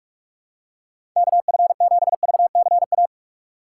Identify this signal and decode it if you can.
Morse code, heard as a keyed tone, about 47 wpm, 700 Hz. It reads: KF7VCA
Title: USA Call Signs - 6 Characters - 8x Character Spacing - 45wpm